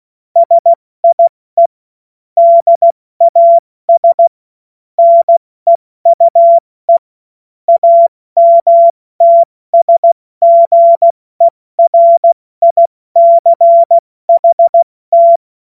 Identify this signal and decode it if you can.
Morse code, heard as a keyed tone, about 16 wpm, 680 Hz. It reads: SIE DAS NEUE AMTSGERICHT
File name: Morsekurs / Witze / Witz 23. Dezember 2025 70 F80